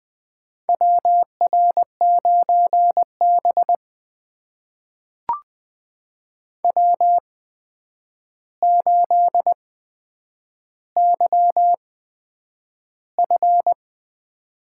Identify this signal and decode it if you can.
Morse code, heard as a keyed tone, about 20 wpm, spaced as for 9 wpm, 700 Hz. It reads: WR9B W8YF
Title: USA Call Signs - 4 Characters - 8x Character Spacing - 20wpm